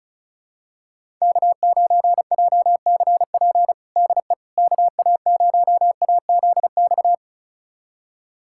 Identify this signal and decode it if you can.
Morse code, heard as a keyed tone, about 34 wpm, 700 Hz. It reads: K9JCP DE KA0AZX
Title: USA Callsign DE Callsign 35wpm